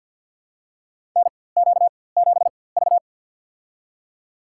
Morse code, heard as a keyed tone, about 53 wpm, spaced as for 24 wpm, 700 Hz. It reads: NQ7V
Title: USA Call Signs - 4x Character Spacing - 50wpm